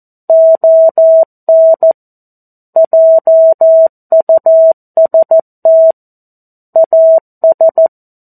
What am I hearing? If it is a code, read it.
Morse code, heard as a keyed tone, about 14 wpm, 650 Hz. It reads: ON JUST AS
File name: 02-Chapter-02 14 WPM